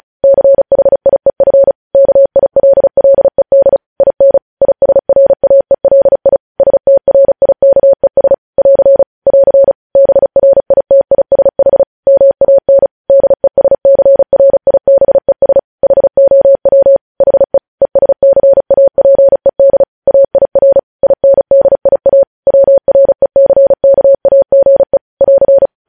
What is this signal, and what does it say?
Morse code, heard as a keyed tone, about 36 wpm, 550 Hz. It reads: CHIEF KILLED IN ISRAELI STRIKES <AR> <AR> BRITISH MAN DESCRIBES HOW HE ESCAPED AIR INDIA WRECKAGE <AR>
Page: news headlines in morse code at 35 wpm